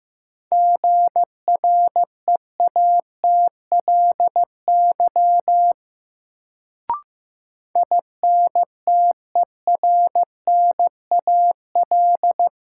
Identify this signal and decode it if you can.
Morse code, heard as a keyed tone, about 15 wpm, 700 Hz. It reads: GREATLY INTERNAL
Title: Sentences from Top 1400 Words - New Word Drill Alt - 15wpm